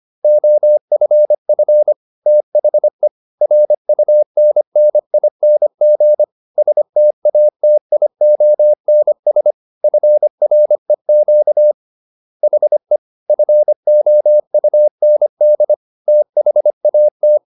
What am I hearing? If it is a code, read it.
Morse code, heard as a keyed tone, about 25 wpm, 600 Hz. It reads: OFF THE RUNNING STATIONS FREQ HE FOUND THAT
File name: SS305 25